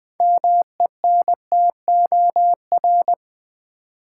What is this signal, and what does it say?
Morse code, heard as a keyed tone, about 20 wpm, 700 Hz. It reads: MENTOR
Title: Build A Word - 3 Letters At A Time - 6-Letter Words - 20wpm